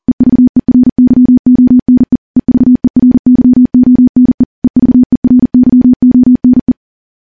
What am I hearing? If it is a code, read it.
Morse code, heard as a keyed tone, about 40 wpm, 250 Hz. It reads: EVERYONE EVERYONE EVERYONE
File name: morse-training-40WPM